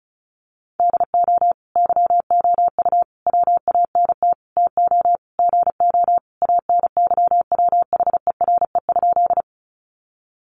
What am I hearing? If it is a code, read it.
Morse code, heard as a keyed tone, about 34 wpm, 700 Hz. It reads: DO YOU WANT TO GO ANYWHERE?
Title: Sentences from Top 1400 Words 35wpm